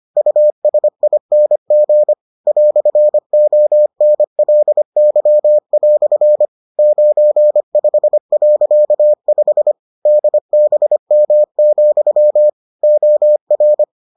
Morse code, heard as a keyed tone, about 25 wpm, 600 Hz. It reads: USING "ONLY" 95.5 DBM, OR